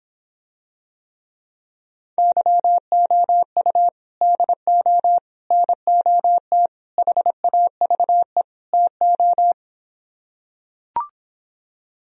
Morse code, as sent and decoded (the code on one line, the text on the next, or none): -.-- --- ..- / -.. --- / -. --- - / .... .- ...- . / - ---
YOU DO NOT HAVE TO